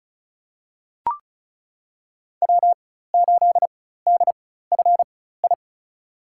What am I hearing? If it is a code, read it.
Morse code, heard as a keyed tone, about 34 wpm, spaced as for 16 wpm, 700 Hz. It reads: W8DFI
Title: USA Call Signs - 4x Character Spacing - 35wpm